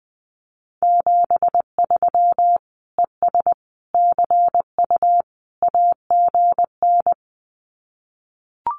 Morse code, heard as a keyed tone, about 20 wpm, 700 Hz. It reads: 73 ES CU AGN
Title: Common QSO Phrases Encore 20wpm